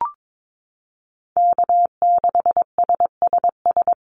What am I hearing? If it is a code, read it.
Morse code, heard as a keyed tone, about 22 wpm, 700 Hz. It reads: K6SSS
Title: Call Signs - 22wpm